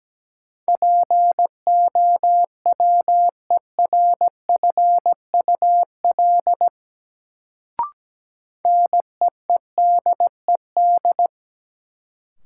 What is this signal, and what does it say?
Morse code, heard as a keyed tone, about 17 wpm, 700 Hz. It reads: POWERFUL NEEDED
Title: Sentences from Top 1000 Words - Review Part 6 - New Word Drill - 17wpm